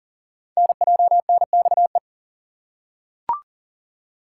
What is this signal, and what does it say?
Morse code, heard as a keyed tone, about 38 wpm, 690 Hz. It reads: NJNXE
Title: Code Groups - 5 Letters - 40wpm